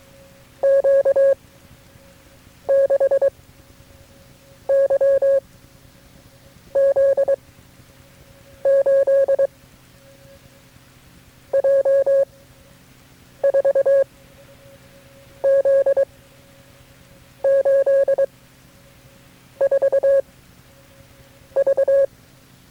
Morse code, heard as a keyed tone, about 23 wpm, 550 Hz. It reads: Q 6 Y Z 8 J 4 Z 8 4 V
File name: KB6MT-tape3-side1